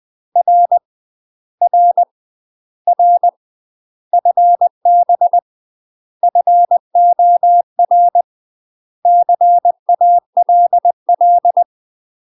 Morse code, heard as a keyed tone, about 20 wpm, 700 Hz. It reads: R R R FB FOR CALL
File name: UK MASTER QSOs TRAINING / 20WPM / UK MASTER QSOs 0034